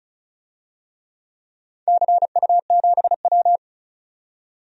Morse code, heard as a keyed tone, about 36 wpm, 700 Hz. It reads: CU7W
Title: DXCC-100 Call Signs - 4 Characters - 35wpm